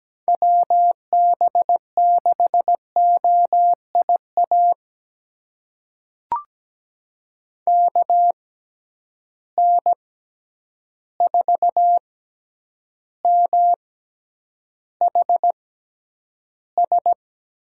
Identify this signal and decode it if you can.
Morse code, heard as a keyed tone, about 17 wpm, spaced as for 9 wpm, 700 Hz. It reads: WB6OIA KN4MHS
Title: USA Call Signs - 6 Characters - 6x Character Spacing - 17wpm